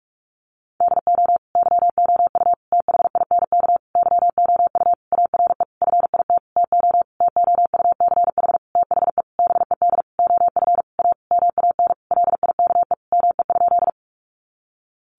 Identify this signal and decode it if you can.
Morse code, heard as a keyed tone, about 44 wpm, 700 Hz. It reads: DO YOU THINK YOU ARE FIT TO TOUCH THE BED OF A MAN LIKE ME?
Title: Sentences from Top 1200 Words 45wpm